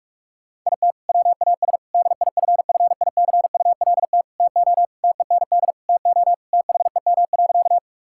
Morse code, heard as a keyed tone, about 46 wpm, 700 Hz. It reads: IT WAS DIFFICULT TO TEND TO THEM.
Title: Sentences from Top 1000 Words - Review Part 5 - 45wpm